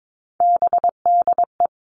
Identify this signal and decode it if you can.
Morse code, heard as a keyed tone, about 22 wpm, 700 Hz. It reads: BDE
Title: All Cut Numbers - 3-Digit 22wpm